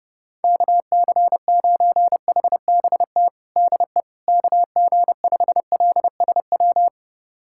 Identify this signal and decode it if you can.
Morse code, heard as a keyed tone, about 32 wpm, 700 Hz. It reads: KC9HBT DE KG5LSW
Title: USA Callsign DE Callsign 30wpm